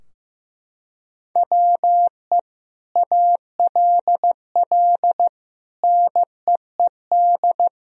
Morse code, heard as a keyed tone, about 15 wpm, 700 Hz. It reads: WE ALL NEED
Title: Sentences from Top 1000 Words - Review Part 3 - 15wpm